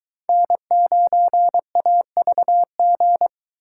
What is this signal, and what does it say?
Morse code, heard as a keyed tone, about 21 wpm, 700 Hz. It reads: N9AVG